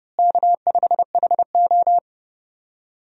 Morse code, heard as a keyed tone, about 30 wpm, 700 Hz. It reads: K5HO
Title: USA Call Signs - 4 Characters - Repeated 3x - 30wpm